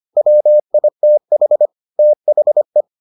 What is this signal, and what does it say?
Morse code, heard as a keyed tone, about 25 wpm, 600 Hz. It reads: WITH THE